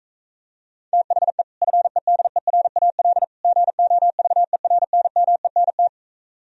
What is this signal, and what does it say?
Morse code, heard as a keyed tone, about 34 wpm, 700 Hz. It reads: THE FEDERAL GOVERNMENT
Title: Speed-Racing - Sets of 3 Words 28wpm